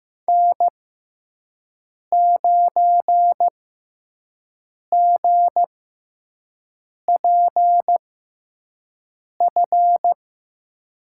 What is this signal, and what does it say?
Morse code, heard as a keyed tone, about 15 wpm, spaced as for 5 wpm, 700 Hz. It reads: N9GPF
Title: USA Call Signs - 5 Characters - 6x Character Spacing - 15wpm